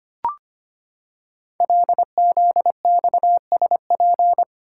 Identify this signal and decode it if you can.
Morse code, heard as a keyed tone, about 25 wpm, 700 Hz. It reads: LZXSP
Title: Code Groups - 5 Letters - 25wpm